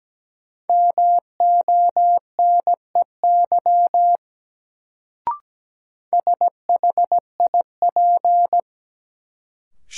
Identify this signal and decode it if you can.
Morse code, heard as a keyed tone, about 17 wpm, 700 Hz. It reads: MONEY SHIP
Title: Sentences from Top 400 Words - New Word Drill - 17wpm